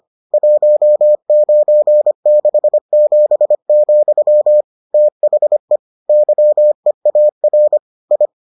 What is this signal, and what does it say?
Morse code, heard as a keyed tone, about 25 wpm, 600 Hz. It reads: 1967, THE YEAR I